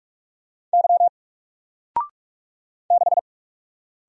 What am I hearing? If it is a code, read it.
Morse code, heard as a keyed tone, about 44 wpm, 700 Hz. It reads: Y 6
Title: Single Character (Letter:Number:Punctuation:Prosign) Alt 45wpm